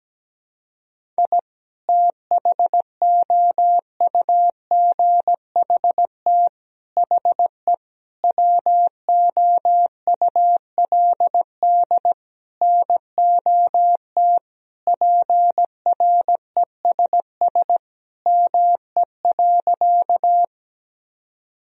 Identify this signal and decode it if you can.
Morse code, heard as a keyed tone, about 17 wpm, 700 Hz. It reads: I THOUGHT HE WOULD NOT PRESS ME.
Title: Sentences from Top 1000 Words - Review Part 1 - 17wpm